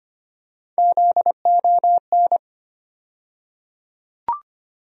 Morse code, heard as a keyed tone, about 25 wpm, 700 Hz. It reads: ZON
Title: USA Call Sign Postfixes - 3 Letters 25wpm